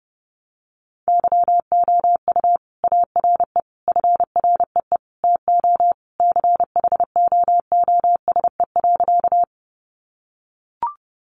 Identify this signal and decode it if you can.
Morse code, heard as a keyed tone, about 30 wpm, 700 Hz. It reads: YOU ARE FREE TO CHOOSE.
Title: Sentences from Top 700 Words 30wpm